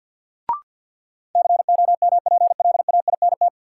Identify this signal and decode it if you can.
Morse code, heard as a keyed tone, about 48 wpm, 700 Hz. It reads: COMPLAINT
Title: Sentences from Top 1400 Words - New Word Drill - 50wpm